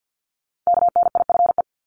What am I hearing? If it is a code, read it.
Morse code, heard as a keyed tone, about 50 wpm, 700 Hz. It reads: KNIFE
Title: Multi-letter Phonemes 50wpm